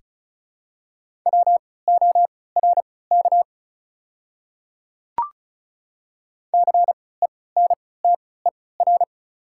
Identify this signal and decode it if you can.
Morse code, heard as a keyed tone, about 34 wpm, spaced as for 20 wpm, 690 Hz. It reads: WORK CENTER
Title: 3x Character Spacing - Top 500 Words - 35wpm